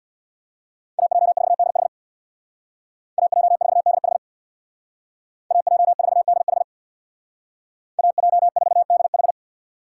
Morse code, heard as a keyed tone, about 64 wpm, 700 Hz. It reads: AJ4DH AJ4DH AJ4DH AJ4DH